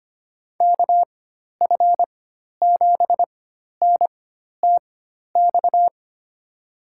Morse code, heard as a keyed tone, about 25 wpm, spaced as for 12 wpm, 700 Hz. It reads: KF7NTX
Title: USA Call Signs - 6 Characters - 4x Character Spacing - 25wpm